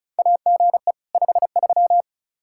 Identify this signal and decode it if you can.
Morse code, heard as a keyed tone, about 36 wpm, 700 Hz. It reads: AGE 53
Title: Practice QSO - Exchange 3 - More Info - 35wpm